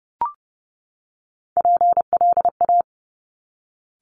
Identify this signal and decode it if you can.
Morse code, heard as a keyed tone, about 30 wpm, 690 Hz. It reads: PLA
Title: Top Three Letter Combinations in Words 30wpm